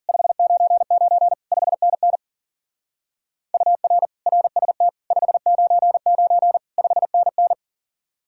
Morse code, heard as a keyed tone, about 49 wpm, 700 Hz. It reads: 599 5NN UR RST 599 5NN
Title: Speed-Racing Common QSO Phrases - 40wpm